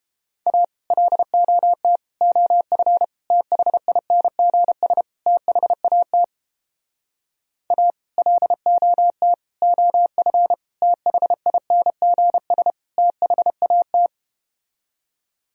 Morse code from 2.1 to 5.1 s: OF THINGS

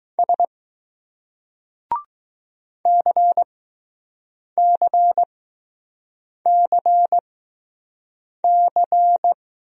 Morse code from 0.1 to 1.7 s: S

